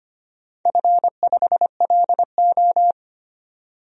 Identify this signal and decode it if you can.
Morse code, heard as a keyed tone, about 25 wpm, 700 Hz. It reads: F5LO
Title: DXCC-100 Call Signs - 4 Characters - 25wpm